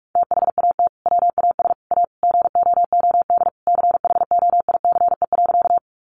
Morse code, heard as a keyed tone, about 44 wpm, 700 Hz. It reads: THAT WAS A GOOD CHOICE.
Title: Sentences from Top 900 Words 45wpm